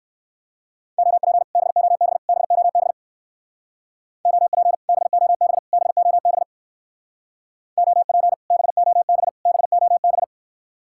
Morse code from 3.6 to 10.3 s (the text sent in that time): OP BOB BOB OP BOB BOB